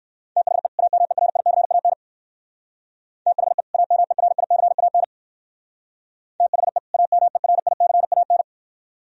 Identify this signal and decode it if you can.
Morse code, heard as a keyed tone, about 53 wpm, 700 Hz. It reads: THE AMERICAN THE AMERICAN THE AMERICAN
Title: Speed-Racing - Sets of 2 Words 45wpm